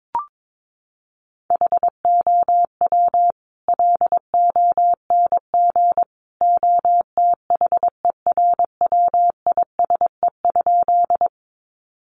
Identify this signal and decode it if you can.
Morse code, heard as a keyed tone, about 22 wpm, 700 Hz. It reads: HOW LONG OTHERWISE?
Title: Sentences from Top 1200 Words 22wpm